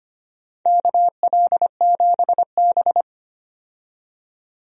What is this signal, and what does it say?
Morse code, heard as a keyed tone, about 25 wpm, 700 Hz. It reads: KL7B
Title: DXCC-100 Call Signs - 4 Characters - 25wpm